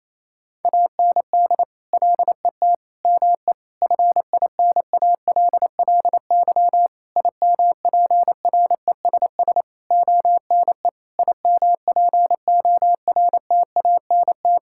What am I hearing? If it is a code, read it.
Morse code, heard as a keyed tone, about 28 wpm, 700 Hz. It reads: AND LET ME FINALLY IMPRESS ONE IMPORTANT
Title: Sentences from Top 1600 Words 28wpm